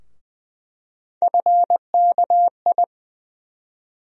Morse code, heard as a keyed tone, about 20 wpm, 700 Hz. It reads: FKI
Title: USA Call Sign Postfixes - 3 Letters 20wpm